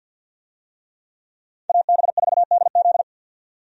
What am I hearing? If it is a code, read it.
Morse code, heard as a keyed tone, about 51 wpm, 700 Hz. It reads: AB4DZ
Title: USA Call Signs - 5 Characters - 50wpm